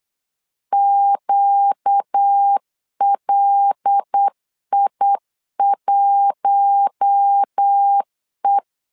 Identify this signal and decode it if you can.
Morse code, heard as a keyed tone, about 8 wpm, 800 Hz. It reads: QLI1E
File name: LEZIONI / 5 LBVC / 5 L 8wpm 2831